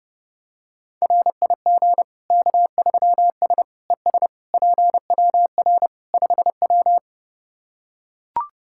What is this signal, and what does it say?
Morse code, heard as a keyed tone, about 30 wpm, 700 Hz. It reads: RIG K3S ES PWR 5W
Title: K1USN SST QSO Practice 30wpm